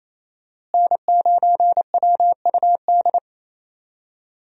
Morse code, heard as a keyed tone, about 27 wpm, 700 Hz. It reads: N9WUD
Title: USA Call Signs - 5 Characters - Repeated 3x - 28wpm